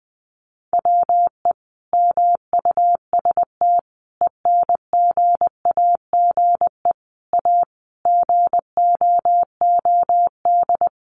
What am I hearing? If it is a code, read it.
Morse code, heard as a keyed tone, about 20 wpm, 700 Hz. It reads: WE MUST ENGAGE A GOOD
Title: Sentences from Top 1500 Words 20wpm